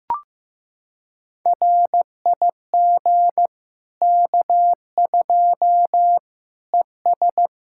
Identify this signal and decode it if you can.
Morse code, heard as a keyed tone, about 15 wpm, 700 Hz. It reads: RIG K2 ES
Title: Practice QSO - Exchange 2 - Info - 15wpm